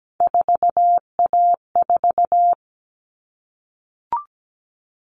Morse code, heard as a keyed tone, about 17 wpm, 700 Hz. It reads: 4A4
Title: All Cut Numbers - 3-Digit 17wpm